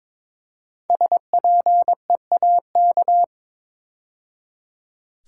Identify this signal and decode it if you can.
Morse code, heard as a keyed tone, about 22 wpm, 700 Hz. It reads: SPEAK
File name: 5 Letter Words - First 3 Letters Given - 22wpm